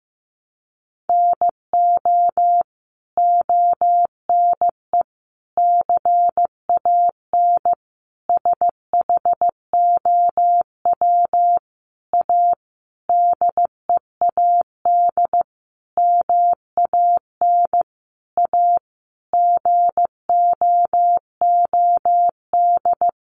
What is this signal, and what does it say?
Morse code, heard as a keyed tone, about 15 wpm, 700 Hz. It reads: NO ONE CAN SHOW A DEAD MAN A GOOD
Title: Sentences from Top 1500 Words 15wpm